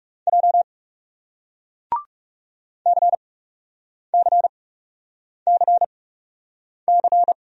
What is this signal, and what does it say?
Morse code, heard as a keyed tone, about 39 wpm, spaced as for 8 wpm, 700 Hz. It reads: J CCCC